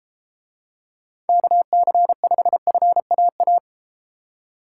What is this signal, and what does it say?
Morse code, heard as a keyed tone, about 32 wpm, 700 Hz. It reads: KC5FAA